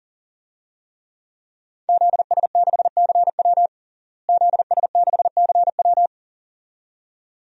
Morse code, heard as a keyed tone, about 40 wpm, 700 Hz. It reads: ZS6CW ZS6CW
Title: DXCC-100 Call Signs - 5 Characters - Repeated 2x - 40wpm